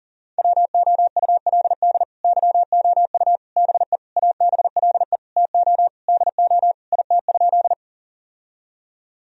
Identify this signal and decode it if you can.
Morse code, heard as a keyed tone, about 40 wpm, 700 Hz. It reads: WOULD YOU BE ABLE TO DO IT?
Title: Sentences from Top 1000 Words - Review Part 2 - 40wpm